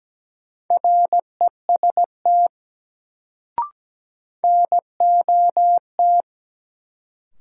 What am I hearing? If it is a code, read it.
Morse code, heard as a keyed tone, about 17 wpm, 700 Hz. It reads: REST NOT
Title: Sets of 2 Words - New Word Drill - 17wpm